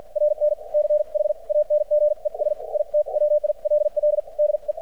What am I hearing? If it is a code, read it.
Morse code, heard as a keyed tone, about 50 wpm, spaced as for 41 wpm, 600 Hz. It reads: RN CHANGES IT WILL DE
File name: archived / CRSnet